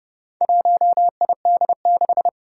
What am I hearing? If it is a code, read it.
Morse code, heard as a keyed tone, about 30 wpm, 690 Hz. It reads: 1ID6
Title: Four Challenging Letter-Number Combinations 30wpm